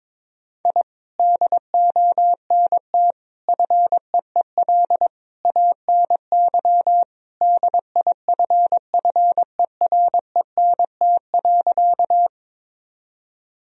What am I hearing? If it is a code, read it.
Morse code, heard as a keyed tone, about 22 wpm, 700 Hz. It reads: I DONT FEEL ANY DIFFERENT.
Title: Sentences from Top 1000 Words - Review Part 3 - 22wpm